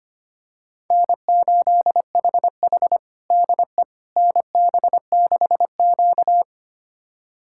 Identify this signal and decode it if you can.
Morse code, heard as a keyed tone, about 25 wpm, 700 Hz. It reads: N8HH DE NB6Q
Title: USA Callsign DE Callsign - 4 Characters - 25wpm